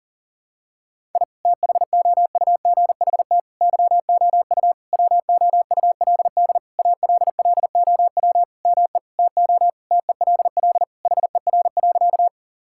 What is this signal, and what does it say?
Morse code, heard as a keyed tone, about 37 wpm, 700 Hz. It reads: I THOUGHT YOU WOULD ALLOW ME TO TELL HER.
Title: Sentences from Top 1000 Words - Review Part 5 - 40wpm